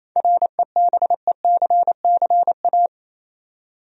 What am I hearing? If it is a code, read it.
Morse code, heard as a keyed tone, about 28 wpm, 700 Hz. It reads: REBECCA
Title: USA Common Names - 28wpm